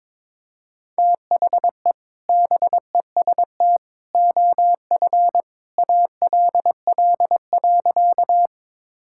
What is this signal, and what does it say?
Morse code, heard as a keyed tone, about 22 wpm, 700 Hz. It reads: THE BEST OF ALL.
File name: practice / 22wpm / Sentences with 2-Word Phrases - 22wpm